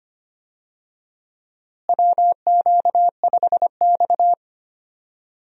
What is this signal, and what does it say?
Morse code, heard as a keyed tone, about 25 wpm, 700 Hz. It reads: WQ5X
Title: USA Call Signs - 4 Characters - Repeated 2x - 25wpm